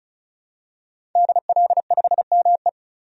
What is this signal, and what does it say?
Morse code, heard as a keyed tone, about 36 wpm, 700 Hz. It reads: DL5ME
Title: DXCC-100 Call Signs - 5 Characters - 35wpm